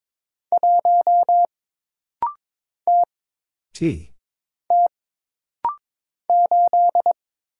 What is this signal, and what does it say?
Morse code, heard as a keyed tone, about 22 wpm, 700 Hz. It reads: 1 T T 8
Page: This practice set is composed of Letters, Numbers, common punctuations, and common prosigns.